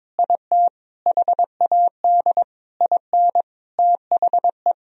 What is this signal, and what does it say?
Morse code, heard as a keyed tone, about 22 wpm, 700 Hz. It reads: IT HAD IN THE
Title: Sentences from Top 1000 Words - Review Part 3 - 22wpm